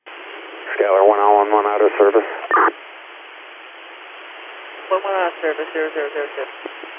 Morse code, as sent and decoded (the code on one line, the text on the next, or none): none